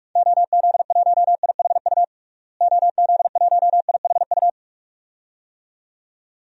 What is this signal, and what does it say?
Morse code, heard as a keyed tone, about 46 wpm, 700 Hz. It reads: OZ1IHU OZ1IHU
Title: DXCC-100 Call Signs - 6 Characters - Repeated 2x - 45wpm